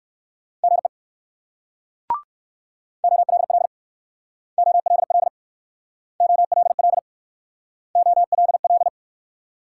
Morse code, heard as a keyed tone, about 56 wpm, 690 Hz. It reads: GE OLL OLL OLL OLL
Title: Speed-Racing Multi-letter Phonemes - 45wpm